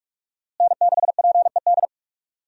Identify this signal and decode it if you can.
Morse code, heard as a keyed tone, about 46 wpm, 700 Hz. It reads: N6PED
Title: USA Call Signs - 5 Characters - Repeated 2x - 45wpm